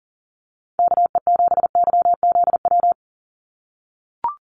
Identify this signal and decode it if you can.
Morse code, heard as a keyed tone, about 40 wpm, 700 Hz. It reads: KE7YZW
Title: USA Call Signs 40wpm